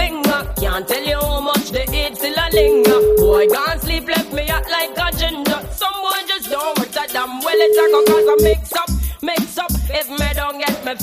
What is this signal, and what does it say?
Ringback tone (Europe, 425 Hz), with noise.